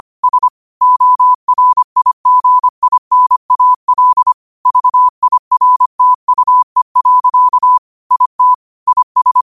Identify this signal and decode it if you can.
Morse code, heard as a keyed tone, about 25 wpm, 1000 Hz. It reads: S ORIGINAL VIRTUE. IT IS
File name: qotdfiles / 25wpm